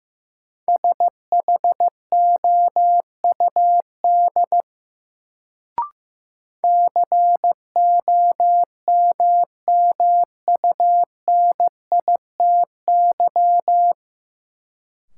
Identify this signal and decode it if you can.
Morse code, heard as a keyed tone, about 15 wpm, 700 Hz. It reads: SHOUD COMMUNITY
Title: Sentences from Top 1000 Words - Review Part 4 - New Word Drill - 15wpm